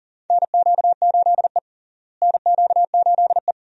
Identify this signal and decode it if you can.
Morse code, heard as a keyed tone, about 37 wpm, 700 Hz. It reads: NQ8E NQ8E
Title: USA Call Signs - 4 Characters - Repeated 2x - 40wpm